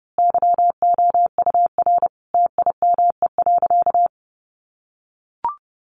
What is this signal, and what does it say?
Morse code, heard as a keyed tone, about 30 wpm, 700 Hz. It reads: YOUR TIME.
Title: Sentences from Top 100 Words 30wpm